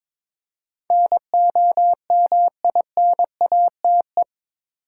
Morse code, heard as a keyed tone, about 22 wpm, 700 Hz. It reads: NOMINATE
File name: practice / 22wpm / Eight Letter Words - 22wpm